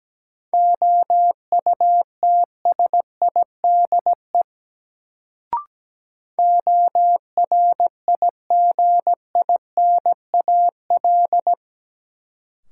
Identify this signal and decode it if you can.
Morse code, heard as a keyed tone, about 17 wpm, 700 Hz. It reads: OUTSIDE ORIGINAL
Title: Sentences from Top 900 Words - New Word Drill - 17wpm